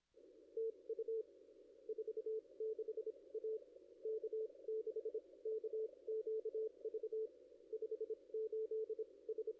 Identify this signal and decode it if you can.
Morse code, heard as a keyed tone, about 30 wpm, spaced as for 23 wpm, 440 Hz. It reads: TU 46 A K6KQV 58 S
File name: contests / ss-cw-k5tr-op-n5rz -1500